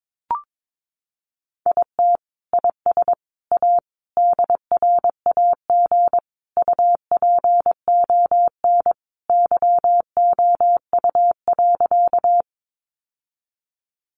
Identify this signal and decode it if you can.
Morse code, heard as a keyed tone, about 22 wpm, 700 Hz. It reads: IT IS A DRAG UPON YOU.
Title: Sentences from Top 1500 Words 22wpm